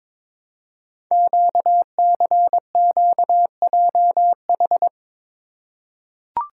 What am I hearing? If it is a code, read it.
Morse code, heard as a keyed tone, about 22 wpm, 700 Hz. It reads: QCQJH
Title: Code Groups - 5 Letters - 22wpm